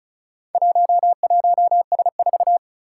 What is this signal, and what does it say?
Morse code, heard as a keyed tone, about 36 wpm, 700 Hz. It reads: 11S4
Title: Four Challenging Letter-Number Combinations 35wpm